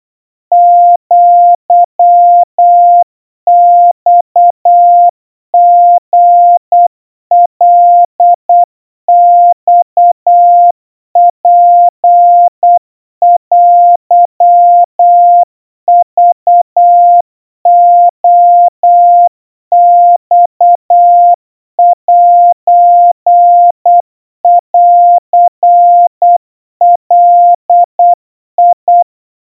Morse code (Japanese Wabun code), heard as a keyed tone, about 8 wpm, 700 Hz. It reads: アマリカマツテクレマセンガ
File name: 30 cpm 700 Hz Crispy